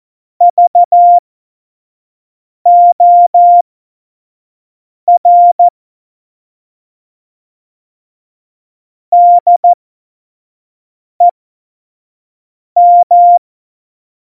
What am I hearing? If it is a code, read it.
Morse code, heard as a keyed tone, about 14 wpm, spaced as for 5 wpm, 700 Hz. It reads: VOR DEM
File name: Morsekurs / Witze / Witz 08. Mai 2018, 25 F70